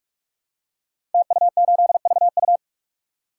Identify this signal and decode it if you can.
Morse code, heard as a keyed tone, about 44 wpm, 700 Hz. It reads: TU8VU
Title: Code Groups - 5 Letter:Numbers - 45wpm